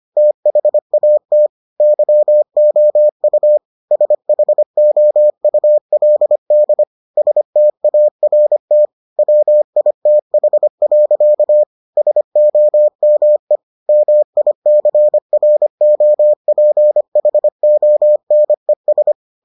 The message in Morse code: THAT YOU SHOULD START WITH. SOME MICROPHONES

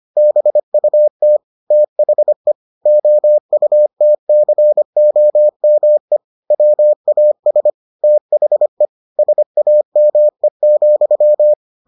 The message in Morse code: BUT THE OUTCOME WAS THE SAME,

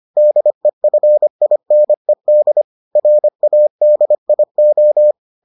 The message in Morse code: DEFINED RADIO

